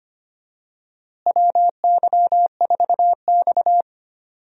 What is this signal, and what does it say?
Morse code, heard as a keyed tone, about 25 wpm, 700 Hz. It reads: WY4X
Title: USA Call Signs - 4 Characters - Repeated 2x - 25wpm